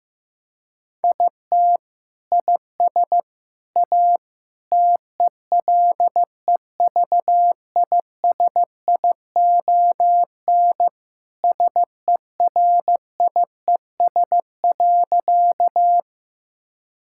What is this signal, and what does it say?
Morse code, heard as a keyed tone, about 16 wpm, 700 Hz. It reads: IT IS A TELEVISION SERIES.
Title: Sentences from Top 1000 Words - Review Part 5 - 15wpm